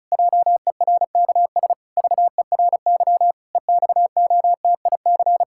Morse code, heard as a keyed tone, about 36 wpm, 700 Hz. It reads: JERKS VERY EXOTIC
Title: Pangram Challenge v2 - 35wpm